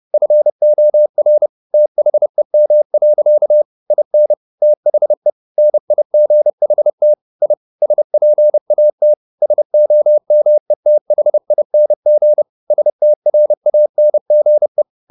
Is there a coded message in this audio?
Yes: Morse code from 0.1 to 14.8 s, heard as a keyed tone, about 30 wpm, 600 Hz. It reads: FOR THEM. IN THE NIGHT I SPAT SOMETHING STRANGE